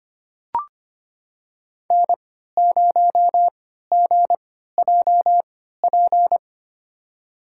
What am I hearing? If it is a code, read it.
Morse code, heard as a keyed tone, about 25 wpm, spaced as for 14 wpm, 700 Hz. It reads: N0GJP
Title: USA Call Signs - 5 Characters - 3x Character Spacing - 25wpm